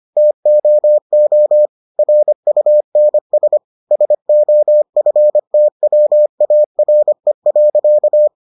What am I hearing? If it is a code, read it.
Morse code, heard as a keyed tone, about 25 wpm, 600 Hz. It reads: TOO RUNS SOFTWARE.